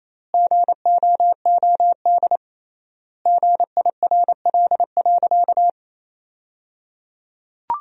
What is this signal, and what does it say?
Morse code, heard as a keyed tone, about 28 wpm, 700 Hz. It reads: GOOD GIRL.
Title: 3x Word-Spacing - Sentences from Top 400 Words - 28wpm